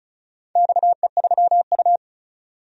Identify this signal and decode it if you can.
Morse code, heard as a keyed tone, about 36 wpm, 700 Hz. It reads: XE3U
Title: DXCC-100 Call Signs - 4 Characters - 35wpm